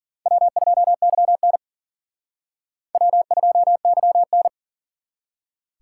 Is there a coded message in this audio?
Yes: Morse code from 0.3 to 4.5 s, heard as a keyed tone, about 52 wpm, 700 Hz. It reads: W2YN W2YN